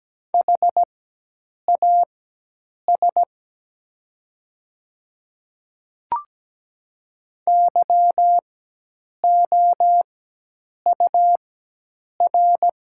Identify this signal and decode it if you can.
Morse code, heard as a keyed tone, about 17 wpm, spaced as for 8 wpm, 700 Hz. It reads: HAS YOUR
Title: 4x Character Spacing - Top 100 Words - 17wpm